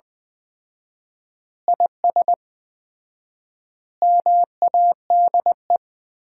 Morse code, heard as a keyed tone, about 20 wpm, 700 Hz. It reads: IS MADE
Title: 4x Word-Spacing - Sentences from Top 500 Words - 20wpm